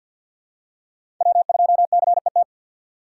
Morse code, heard as a keyed tone, about 53 wpm, 700 Hz. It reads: W2CET
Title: USA Call Signs - 5 Characters - 3x Character Spacing - 50wpm